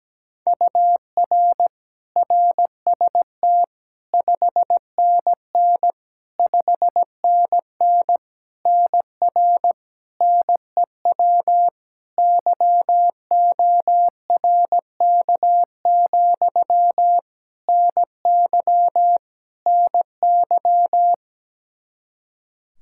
Morse code, heard as a keyed tone, about 17 wpm, 700 Hz. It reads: UR RST 5NN 5NN NR NEW YORK, NY NY
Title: K1USN SST QSO Practice 17wpm